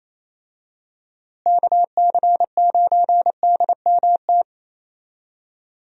Morse code, heard as a keyed tone, about 28 wpm, 700 Hz. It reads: KC9DMT
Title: USA Call Signs - 6 Characters - Repeated 2x - 28wpm